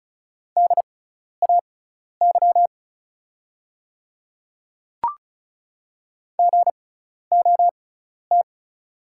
Morse code, heard as a keyed tone, about 36 wpm, spaced as for 12 wpm, 690 Hz. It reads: DAY GOT
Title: 6x Character Spacing - Top 500 Words - 35wpm